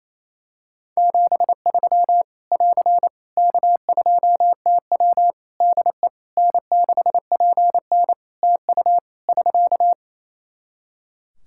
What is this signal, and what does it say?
Morse code, heard as a keyed tone, about 28 wpm, 700 Hz. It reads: 73 <AR> K2TW DE N6PN TU <SK>
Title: Practice QSO - Exchange 4 - Ending - 28wpm